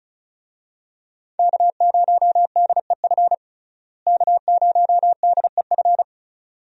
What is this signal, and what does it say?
Morse code, heard as a keyed tone, about 35 wpm, 700 Hz. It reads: K0DEF K0DEF
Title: USA Call Signs - 5 Characters - Repeated 2x - 35wpm